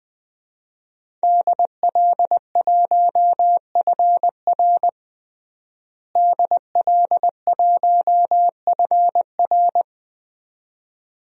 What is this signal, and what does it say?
Morse code, heard as a keyed tone, about 20 wpm, 700 Hz. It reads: DL1FR DL1FR
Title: DXCC-100 Call Signs - 5 Characters - Repeated 2x - 20wpm